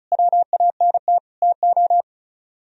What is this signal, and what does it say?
Morse code, heard as a keyed tone, about 44 wpm, 700 Hz. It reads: WANT TO
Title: Speed-Racing - Sets of 4 Words 35wpm